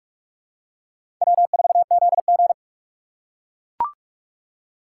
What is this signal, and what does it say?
Morse code, heard as a keyed tone, about 44 wpm, 700 Hz. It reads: W4ZG
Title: USA Call Signs - 6x Character Spacing - 45wpm